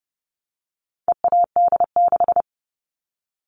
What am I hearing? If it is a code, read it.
Morse code, heard as a keyed tone, about 30 wpm, 700 Hz. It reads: EAD6
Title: All Cut Numbers - 4-Digit 30wpm